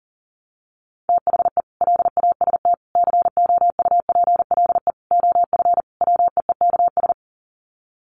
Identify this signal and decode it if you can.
Morse code, heard as a keyed tone, about 40 wpm, 700 Hz. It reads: THE LAST COUPLE OF WEEKS
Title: Sets of 5 Words Encore2 40wpm